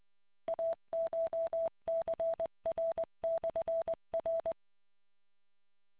Morse code, heard as a keyed tone, about 20 wpm, 660 Hz. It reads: ACHCR/R